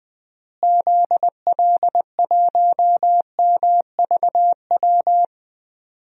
Morse code, heard as a keyed tone, about 20 wpm, 700 Hz. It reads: ZL1MVW
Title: DXCC-100 Call Signs - 6 Characters - Repeated 3x - 20wpm